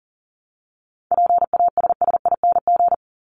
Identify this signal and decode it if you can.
Morse code, heard as a keyed tone, about 40 wpm, 700 Hz. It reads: PASSING
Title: ING Suffix - Top 100 Words with Base Words 40wpm